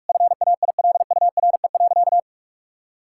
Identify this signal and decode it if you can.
Morse code, heard as a keyed tone, about 46 wpm, 700 Hz. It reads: FAILURE.
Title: Sentences from Top 1000 Words - Review Part 6 - 45wpm